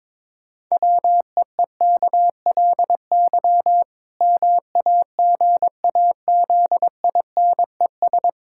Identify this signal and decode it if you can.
Morse code, heard as a keyed tone, about 22 wpm, 700 Hz. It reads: WEEKLY MAGAZINES
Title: Pangram Challenge v2 - 22wpm